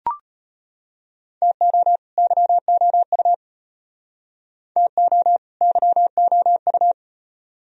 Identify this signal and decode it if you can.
Morse code, heard as a keyed tone, about 33 wpm, 700 Hz. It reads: TO YOU TO YOU